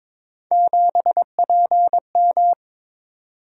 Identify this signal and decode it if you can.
Morse code, heard as a keyed tone, about 22 wpm, 700 Hz. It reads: 7PM
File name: practice / 22wpm / Three Letter-Number Combinations - 22wpm